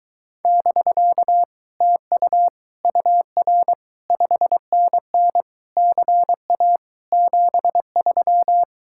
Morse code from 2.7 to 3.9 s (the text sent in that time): UR